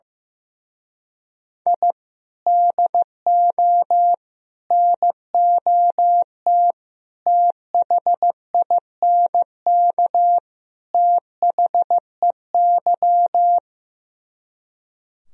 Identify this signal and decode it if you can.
Morse code, heard as a keyed tone, about 15 wpm, 700 Hz. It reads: I DO NOT THINK THEY
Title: Speed-Racing - Sets of 5 Words 15wpm